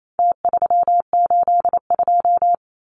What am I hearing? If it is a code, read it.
Morse code, heard as a keyed tone, about 28 wpm, 700 Hz. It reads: T382
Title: Numbers - Contest Style - 4-Digit 28wpm